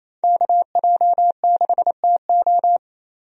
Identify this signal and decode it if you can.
Morse code, heard as a keyed tone, about 28 wpm, 700 Hz. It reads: KJ6TO
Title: USA Call Signs - 5 Characters - 28wpm